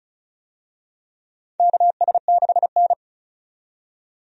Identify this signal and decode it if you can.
Morse code, heard as a keyed tone, about 36 wpm, 700 Hz. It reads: KS6N
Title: USA Call Signs - 4 Characters - 35wpm